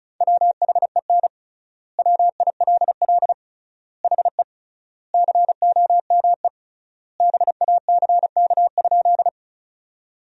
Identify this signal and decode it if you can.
Morse code, heard as a keyed tone, about 36 wpm, 700 Hz. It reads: WHEN WILL HE COME BACK?
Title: 3x Word-Spacing - Sentences from Top 200 Words - 35wpm